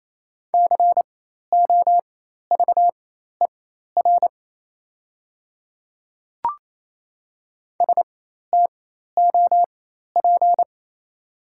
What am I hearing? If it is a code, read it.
Morse code, heard as a keyed tone, about 28 wpm, spaced as for 13 wpm, 700 Hz. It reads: COVER STOP
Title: 4x Character Spacing - Top 300 Words - 28wpm